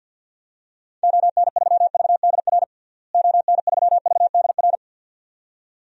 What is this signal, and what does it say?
Morse code, heard as a keyed tone, about 52 wpm, 700 Hz. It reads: ON3VDR ON3VDR
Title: DXCC-100 Call Signs - 6 Characters - Repeated 2x - 50wpm